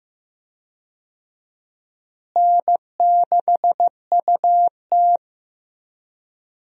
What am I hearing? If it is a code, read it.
Morse code, heard as a keyed tone, about 15 wpm, 700 Hz. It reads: N6UT
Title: USA Call Signs - 4 Characters - Repeated 2x - 15wpm